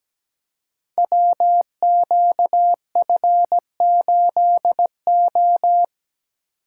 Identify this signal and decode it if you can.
Morse code, heard as a keyed tone, about 17 wpm, 700 Hz. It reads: WQF8O
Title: Code Groups - 5 Letter:Numbers - 17wpm